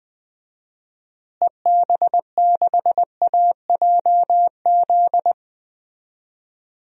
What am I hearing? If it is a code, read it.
Morse code, heard as a keyed tone, about 20 wpm, 700 Hz. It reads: EB6AJZ
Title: DXCC-100 Call Signs - 6 Characters - Repeated 3x - 20wpm